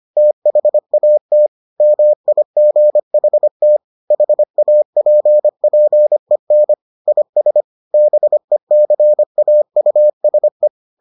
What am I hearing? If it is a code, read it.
Morse code, heard as a keyed tone, about 25 wpm, 600 Hz. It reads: THAT MIGHT HAPPEN IS BECAUSE